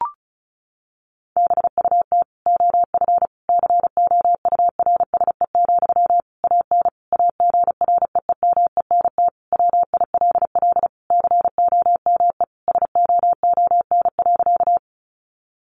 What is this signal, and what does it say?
Morse code, heard as a keyed tone, about 34 wpm, 700 Hz. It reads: BUT OF COURSE, AN AGREEMENT WILL COME SOON.
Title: Sentences from Top 1500 Words 35wpm